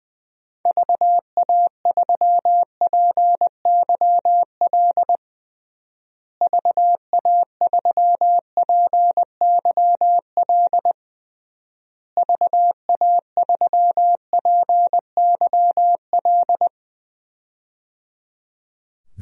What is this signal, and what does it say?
Morse code, heard as a keyed tone, about 20 wpm, 700 Hz. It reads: VA3PYL VA3PYL VA3PYL
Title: DXCC-100 Call Signs - 6 Characters - Repeated 3x - 20wpm